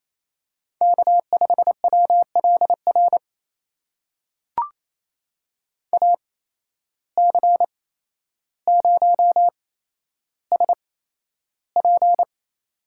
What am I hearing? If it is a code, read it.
Morse code, heard as a keyed tone, about 28 wpm, spaced as for 12 wpm, 700 Hz. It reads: K5WLR AC0SP
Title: USA Call Signs - 5 Characters - 8x Character Spacing - 28wpm